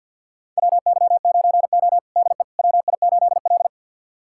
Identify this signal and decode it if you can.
Morse code, heard as a keyed tone, about 53 wpm, 700 Hz. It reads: WY9O DE WI8L
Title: USA Callsign DE Callsign - 4 Characters - 50wpm